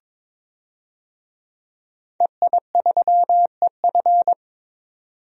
Morse code, heard as a keyed tone, about 22 wpm, 700 Hz. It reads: EI3EF
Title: DXCC-100 Call Signs - 5 Characters - 22wpm